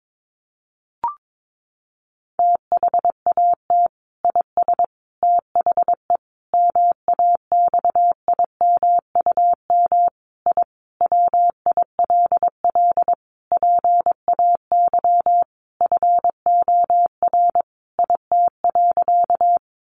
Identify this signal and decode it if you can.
Morse code, heard as a keyed tone, about 22 wpm, 700 Hz. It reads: THAT IS THE MAXIMUM I WILL PAY FOR IT.
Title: Sentences from Top 1300 Words 22wpm